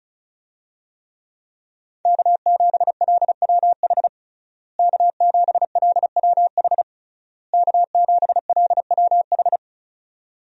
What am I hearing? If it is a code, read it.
Morse code, heard as a keyed tone, about 36 wpm, 700 Hz. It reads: K7LWH K7LWH K7LWH
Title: USA Call Signs - 5 Characters - Repeated 3x - 35wpm